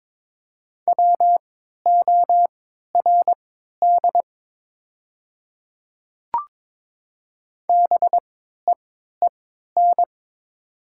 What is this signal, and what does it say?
Morse code, heard as a keyed tone, about 22 wpm, spaced as for 13 wpm, 700 Hz. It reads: WORD BEEN
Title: 3x Character Spacing - Top 200 Words - 22wpm